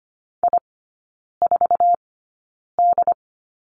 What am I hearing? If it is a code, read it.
Morse code, heard as a keyed tone, about 25 wpm, spaced as for 9 wpm, 700 Hz. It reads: I4D
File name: Lesson 15e - Number 5 - Three Characters 25wpm